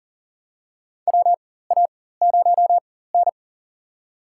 Morse code, heard as a keyed tone, about 40 wpm, spaced as for 19 wpm, 700 Hz. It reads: WA0N
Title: USA Call Signs - 4 Characters - 4x Character Spacing - 40wpm